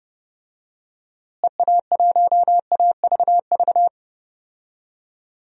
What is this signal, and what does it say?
Morse code, heard as a keyed tone, about 30 wpm, 700 Hz. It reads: EA1AVV
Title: DXCC-100 Call Signs - 6 Characters - Repeated 3x - 30wpm